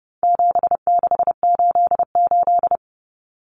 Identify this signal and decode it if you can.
Morse code, heard as a keyed tone, about 30 wpm, 700 Hz. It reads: 7688
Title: Common Cut Numbers - 4-Digit 30wpm